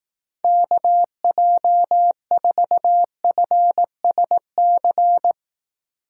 Morse code, heard as a keyed tone, about 18 wpm, 700 Hz. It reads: KJ4FSC
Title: Speed Racing - USA 2x3 Call Signs 15wpm